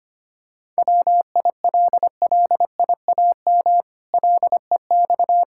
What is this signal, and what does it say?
Morse code, heard as a keyed tone, about 25 wpm, 700 Hz. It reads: WILLIAM LEX
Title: Pangram Challenge v2 - 25wpm